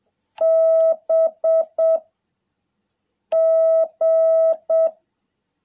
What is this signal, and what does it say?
Morse code, heard as a keyed tone, about 7 wpm, 660 Hz. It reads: B G
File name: jh8cbh / 2021cw